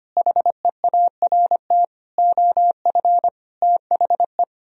Morse code, heard as a keyed tone, about 32 wpm, 700 Hz. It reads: HEART OF THE